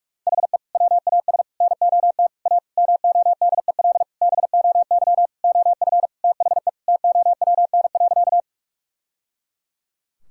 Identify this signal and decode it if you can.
Morse code, heard as a keyed tone, about 48 wpm, 700 Hz. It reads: HE WAS NOT A MODEL BOY OF THE TOWN.
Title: Sentences from Top 1000 Words - Review Part 5 - 45wpm